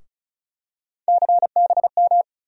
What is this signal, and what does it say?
Morse code, heard as a keyed tone, about 36 wpm, 700 Hz. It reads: CBM
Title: USA Call Sign Postfixes - 3 Letters 35wpm